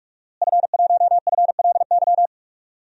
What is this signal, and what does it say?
Morse code, heard as a keyed tone, about 46 wpm, 700 Hz. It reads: F1FLY
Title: DXCC-100 Call Signs - 5 Characters - Repeated 3x - 45wpm